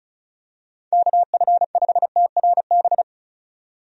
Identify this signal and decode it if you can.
Morse code, heard as a keyed tone, about 36 wpm, 700 Hz. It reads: KF5TRB
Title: USA Call Signs - 6 Characters - 35wpm